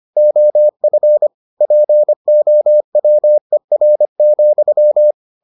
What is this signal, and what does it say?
Morse code, heard as a keyed tone, about 25 wpm, 600 Hz. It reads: OF POWER,